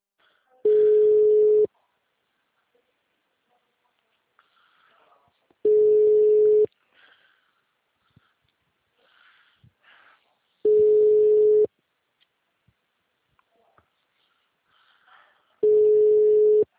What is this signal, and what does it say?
Ringback tone (Europe, 425 Hz), with speech.